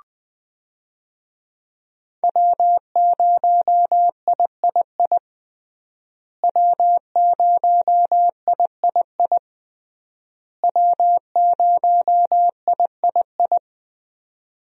Morse code, heard as a keyed tone, about 21 wpm, 700 Hz. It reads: W0III W0III W0III
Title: USA Call Signs - 5 Characters - Repeated 3x - 20wpm